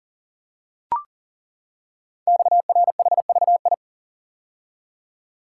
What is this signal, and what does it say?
Morse code, heard as a keyed tone, about 40 wpm, 700 Hz. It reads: XRHVI
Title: Code Groups - 5 Letters - 40wpm